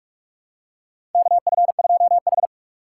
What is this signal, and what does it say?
Morse code, heard as a keyed tone, about 46 wpm, 690 Hz. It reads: KF2H
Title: USA Call Signs - 4 Characters - 8x Character Spacing - 45wpm